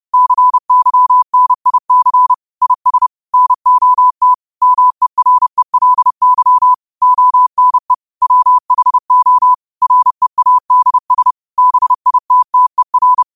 Morse code, heard as a keyed tone, about 30 wpm, 1000 Hz. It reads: CYNIC IS NOT MERELY ONE WHO READS BITTER